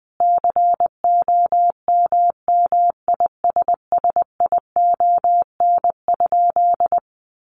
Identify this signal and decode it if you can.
Morse code, heard as a keyed tone, about 20 wpm, 700 Hz. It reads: COMMISSION?
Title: Sentences from Top 1400 Words 20wpm